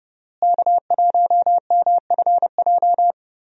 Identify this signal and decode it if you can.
Morse code, heard as a keyed tone, about 30 wpm, 700 Hz. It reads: K1MFJ
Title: USA Call Signs - 5 Characters - Repeated 3x - 30wpm